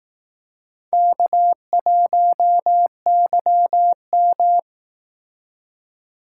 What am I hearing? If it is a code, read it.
Morse code, heard as a keyed tone, about 18 wpm, 700 Hz. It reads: K1YM
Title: Speed Racing - USA 1x2 Call Signs 15wpm